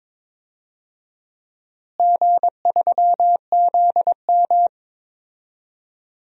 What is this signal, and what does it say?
Morse code, heard as a keyed tone, about 22 wpm, 700 Hz. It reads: G3ZM
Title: DXCC-100 Call Signs - 4 Characters - Repeated 3x - 22wpm